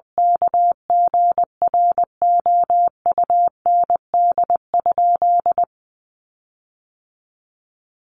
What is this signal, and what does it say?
Morse code, heard as a keyed tone, about 20 wpm, 700 Hz. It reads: KGROUND?
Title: Sentences from Top 1300 Words 20wpm